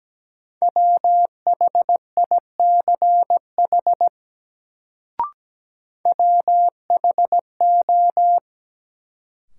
Morse code, heard as a keyed tone, about 17 wpm, 700 Hz. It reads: WHICH WHO
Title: Sets of 2 Words - New Word Drill - 17wpm